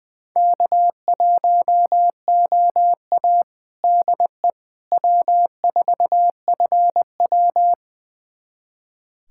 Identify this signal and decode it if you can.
Morse code, heard as a keyed tone, about 21 wpm, 700 Hz. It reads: K1OA DE W4FW
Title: USA Callsign DE Callsign - 4 Characters - 20wpm